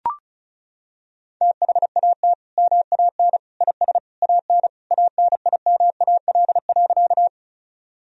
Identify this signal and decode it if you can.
Morse code, heard as a keyed tone, about 36 wpm, 700 Hz. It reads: THAT MAN IS AN ANIMAL.
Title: Sentences from Top 1000 Words - Review Part 1 - 35wpm